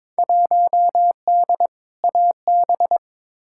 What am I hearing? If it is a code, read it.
Morse code, heard as a keyed tone, about 23 wpm, 700 Hz. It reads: 1D AB
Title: Field Day Exchanges Encore - 22wpm